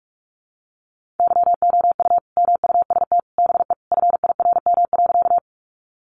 Morse code, heard as a keyed tone, about 44 wpm, 700 Hz. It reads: YOU MUST BE FIRM.
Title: Sentences from Top 1400 Words 45wpm